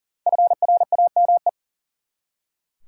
Morse code, heard as a keyed tone, about 40 wpm, 700 Hz. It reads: FRAME
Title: Five Letter Words - 40wpm